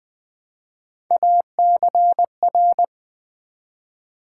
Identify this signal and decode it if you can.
Morse code, heard as a keyed tone, about 20 wpm, 700 Hz. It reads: ACR
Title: Build A Word - 1 Letter At A Time - 4-Letter Words - 20wpm